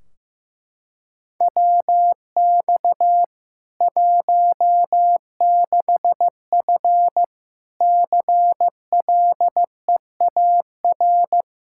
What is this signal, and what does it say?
Morse code, heard as a keyed tone, about 16 wpm, 700 Hz. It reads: WX 16F CLEAR
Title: Practice QSO - Exchange 2 - Info - 15wpm